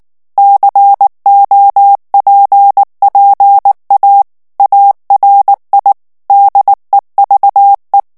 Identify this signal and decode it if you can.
Morse code, heard as a keyed tone, about 19 wpm, 800 Hz. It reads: COPPA ARI DEVE
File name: morse mp3 / ITALIEN texte clair 2 vitesse 18 mots minute